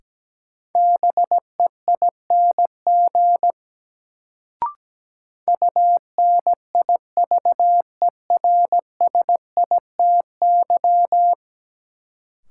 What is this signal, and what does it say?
Morse code, heard as a keyed tone, about 17 wpm, 700 Hz. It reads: BEING UNIVERSITY
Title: Sets of 4 Words - New Word Drill - 17wpm